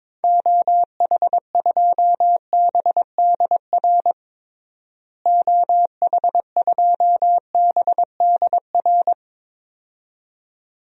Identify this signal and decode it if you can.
Morse code, heard as a keyed tone, about 21 wpm, 700 Hz. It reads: OH2BDR OH2BDR
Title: DXCC-100 Call Signs - 6 Characters - Repeated 2x - 22wpm